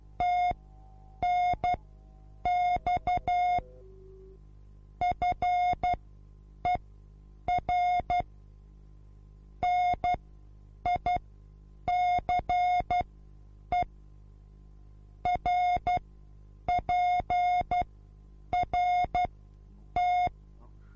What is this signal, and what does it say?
Morse code, heard as a keyed tone, about 12 wpm, spaced as for 8 wpm, 720 Hz. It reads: TNX FER NICE RPRT
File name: repos / Morsekurs / DJ4UF Morsen24